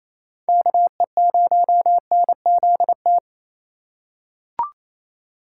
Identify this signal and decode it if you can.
Morse code, heard as a keyed tone, about 28 wpm, 700 Hz. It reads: KE0NZT
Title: USA Call Signs - 6 Characters - 4x Character Spacing - 28wpm